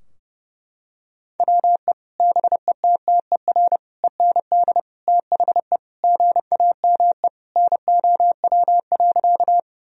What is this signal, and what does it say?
Morse code, heard as a keyed tone, about 30 wpm, 700 Hz. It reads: WE BETTER END THE GAME NOW.
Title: Sentences from Top 1000 Words - Review Part 3 - 30wpm